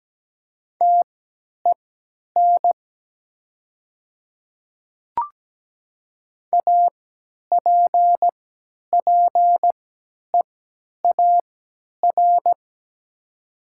Morse code, heard as a keyed tone, about 17 wpm, spaced as for 10 wpm, 700 Hz. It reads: TEN APPEAR
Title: 3x Character Spacing - Top 500 Words - 17wpm